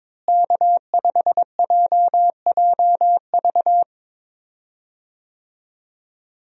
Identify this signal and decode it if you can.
Morse code, heard as a keyed tone, about 22 wpm, 700 Hz. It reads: K5JJV
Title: USA Call Signs - 5 Characters - Repeated 2x - 22wpm